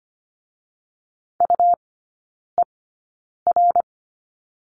Morse code, heard as a keyed tone, about 25 wpm, spaced as for 9 wpm, 700 Hz. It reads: UER
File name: Lesson 17e - Letter u - Three Characters 25wpm